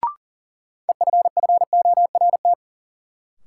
Morse code, heard as a keyed tone, about 40 wpm, 700 Hz. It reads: EFFORT
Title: Sentences from Top 1200 Words - New Word Drill Alt - 40wpm